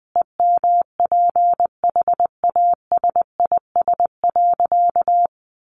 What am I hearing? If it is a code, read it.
Morse code, heard as a keyed tone, about 19 wpm, 700 Hz. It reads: EMPHASIS.
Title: Sentences from Top 1400 Words 20wpm